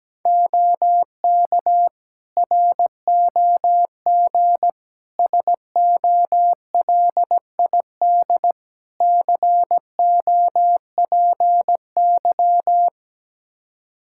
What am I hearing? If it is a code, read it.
Morse code, heard as a keyed tone, about 17 wpm, 700 Hz. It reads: OK ROG SOLID COPY
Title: Practice QSO - Exchange 3 - More Info - 17wpm